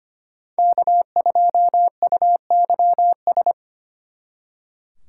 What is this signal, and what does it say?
Morse code, heard as a keyed tone, about 33 wpm, 700 Hz. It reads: K2UYS